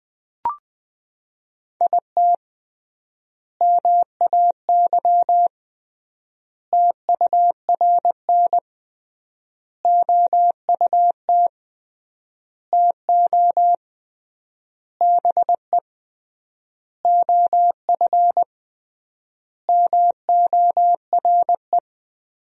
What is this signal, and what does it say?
Morse code, heard as a keyed tone, about 20 wpm, 700 Hz. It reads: IT MAY TURN OUT TO BE OF MORE
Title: 3x Word-Spacing - Sentences from Top 400 Words - 20wpm